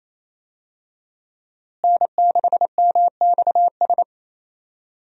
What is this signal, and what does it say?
Morse code, heard as a keyed tone, about 28 wpm, 700 Hz. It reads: N6MXS
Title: USA Call Signs - 5 Characters - Repeated 3x - 28wpm